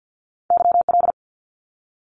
Morse code, heard as a keyed tone, about 50 wpm, 700 Hz. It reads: YL
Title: Abbreviations 50wpm